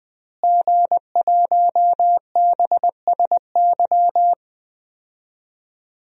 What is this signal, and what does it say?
Morse code, heard as a keyed tone, about 21 wpm, 700 Hz. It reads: G1BSY
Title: DXCC-100 Call Signs - 5 Characters - Repeated 3x - 20wpm